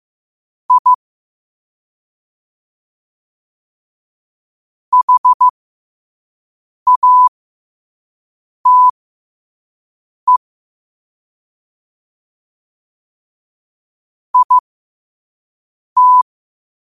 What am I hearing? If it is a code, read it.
Morse code, heard as a keyed tone, about 15 wpm, spaced as for 5 wpm, 1000 Hz. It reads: I HATE IT
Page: New quotes every day in morse code at 5 Words per minute.